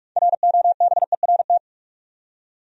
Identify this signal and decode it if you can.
Morse code, heard as a keyed tone, about 46 wpm, 700 Hz. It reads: ROBERT
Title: USA Common Names - 45wpm